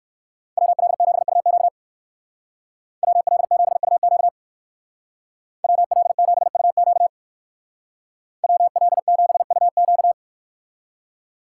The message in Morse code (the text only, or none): WL7UQ WL7UQ WL7UQ WL7UQ